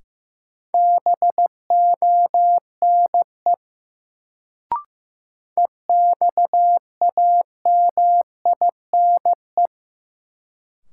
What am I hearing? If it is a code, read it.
Morse code, heard as a keyed tone, about 15 wpm, 700 Hz. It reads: BONE EXAMINE
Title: Sentences from Top 1600 Words - New Word Drill - 15wpm